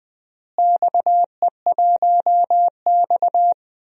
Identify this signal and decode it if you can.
Morse code, heard as a keyed tone, about 20 wpm, 700 Hz. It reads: XE1X
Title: DXCC-100 Call Signs - 4 Characters - Repeated 3x - 20wpm